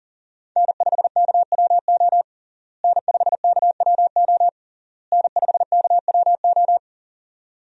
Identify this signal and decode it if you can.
Morse code, heard as a keyed tone, about 40 wpm, 700 Hz. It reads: N5KWO N5KWO N5KWO
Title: USA Call Signs - 5 Characters - Repeated 3x - 40wpm